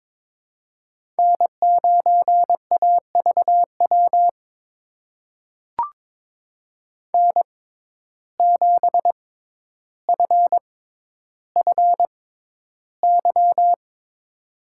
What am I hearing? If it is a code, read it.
Morse code, heard as a keyed tone, about 21 wpm, spaced as for 11 wpm, 700 Hz. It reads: N9AVW N7FFY
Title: USA Call Signs - 5 Characters - 6x Character Spacing - 22wpm